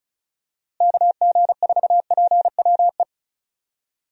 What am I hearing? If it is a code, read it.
Morse code, heard as a keyed tone, about 36 wpm, 700 Hz. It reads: KG4PWE
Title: USA Call Signs - 6 Characters - Repeated 3x - 35wpm